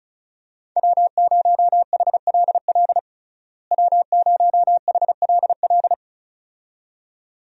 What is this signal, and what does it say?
Morse code, heard as a keyed tone, about 36 wpm, 700 Hz. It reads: W0HLL W0HLL
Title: USA Call Signs - 5 Characters - Repeated 2x - 35wpm